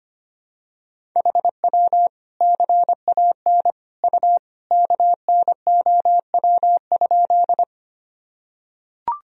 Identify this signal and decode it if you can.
Morse code, heard as a keyed tone, about 25 wpm, 700 Hz. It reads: HW CAN U KNOW?
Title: Sentences from Top 100 Words and Abbreviations 25wpm